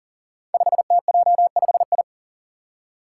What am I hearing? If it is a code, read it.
Morse code, heard as a keyed tone, about 40 wpm, 700 Hz. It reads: 5TJ5I
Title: Five Challenging Letter-Number Combinations 40wpm